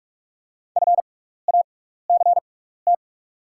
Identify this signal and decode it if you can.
Morse code, heard as a keyed tone, about 44 wpm, spaced as for 15 wpm, 700 Hz. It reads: FACT
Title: 6x Character Spacing - Top 500 Words - 45wpm